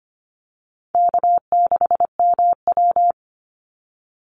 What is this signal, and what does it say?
Morse code, heard as a keyed tone, about 25 wpm, 700 Hz. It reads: K6MW
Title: NA SOTA Chaser Callsigns 25wpm